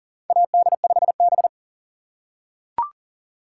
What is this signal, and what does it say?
Morse code, heard as a keyed tone, about 40 wpm, 700 Hz. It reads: AD5B
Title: USA Call Signs - 4 Characters - 6x Character Spacing - 40wpm